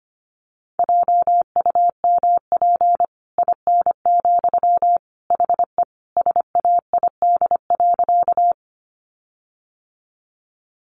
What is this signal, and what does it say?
Morse code, heard as a keyed tone, about 25 wpm, 700 Hz. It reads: JUMP IN, HE SAID.
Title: Sentences from Top 1300 Words 25wpm